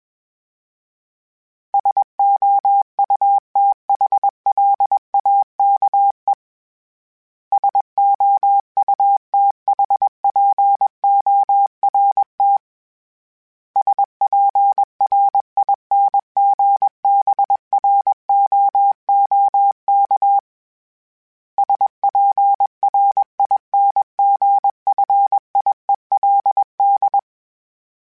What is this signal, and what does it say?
Morse code, heard as a keyed tone, about 21 wpm, 800 Hz. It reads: SOUTHLAKE SOUTHPORT SPRINGBROOK SPRINGFIELD
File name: fromSMs / FromSMs mp3s / Cities@20